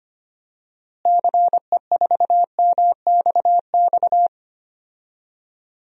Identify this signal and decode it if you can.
Morse code, heard as a keyed tone, about 25 wpm, 700 Hz. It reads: CE4MXX
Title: DXCC-100 Call Signs - 6 Characters - Repeated 2x - 25wpm